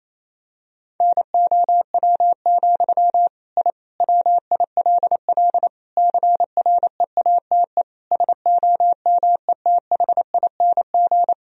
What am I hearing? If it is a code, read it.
Morse code, heard as a keyed tone, about 28 wpm, 700 Hz. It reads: NOW, I WILL CREATE SOMETHING
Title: Sentences from Top 1000 Words - Review Part 3 - 28wpm